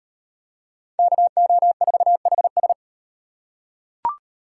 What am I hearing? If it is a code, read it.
Morse code, heard as a keyed tone, about 35 wpm, 700 Hz. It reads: KO4HS